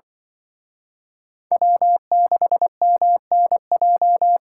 Morse code, heard as a keyed tone, about 25 wpm, 700 Hz. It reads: W6MNJ